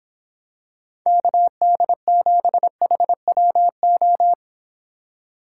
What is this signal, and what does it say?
Morse code, heard as a keyed tone, about 26 wpm, 700 Hz. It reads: KD7HWO